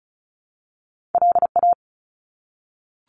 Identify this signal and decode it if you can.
Morse code, heard as a keyed tone, about 36 wpm, 700 Hz. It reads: LA
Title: US States 35wpm